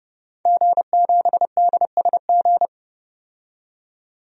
Morse code, heard as a keyed tone, about 30 wpm, 700 Hz. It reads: G7DSG
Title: DXCC-100 Call Signs - 5 Characters - 30wpm